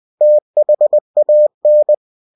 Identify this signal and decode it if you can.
Morse code, heard as a keyed tone, about 20 wpm, 600 Hz. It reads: THAN
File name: cw-books / winnie-the-pooh / winnie-20wpm-page-0093